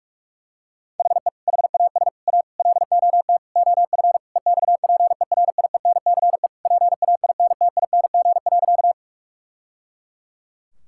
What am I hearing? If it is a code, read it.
Morse code, heard as a keyed tone, about 46 wpm, 700 Hz. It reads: HE HAS A LOT OF EXPERIENCE PAINTING.
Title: Sentences from Top 1000 Words - Review Part 6 - 45wpm